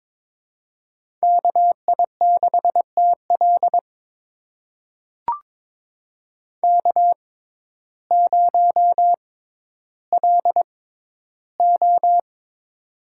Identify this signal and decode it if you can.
Morse code, heard as a keyed tone, about 22 wpm, spaced as for 12 wpm, 700 Hz. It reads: KI6TL K0LO
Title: USA Call Signs - 6x Character Spacing - 22wpm